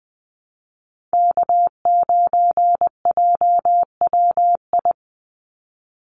Morse code, heard as a keyed tone, about 20 wpm, 700 Hz. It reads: K9JWI
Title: Call Signs Repeated 2x - 20wpm